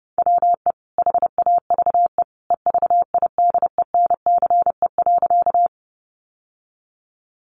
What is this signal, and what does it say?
Morse code, heard as a keyed tone, about 30 wpm, 700 Hz. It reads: WE HAVE EVIDENCE.
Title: Sentences from Top 1300 Words 30wpm